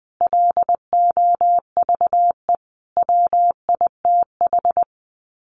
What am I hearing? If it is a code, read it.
Morse code, heard as a keyed tone, about 20 wpm, 700 Hz. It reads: LOVE WITH
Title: Sets of 5 Words Encore 20wpm